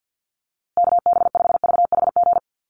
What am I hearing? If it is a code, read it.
Morse code, heard as a keyed tone, about 50 wpm, 700 Hz. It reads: KB5VHG
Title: USA Call Signs Repeated 3x 50wpm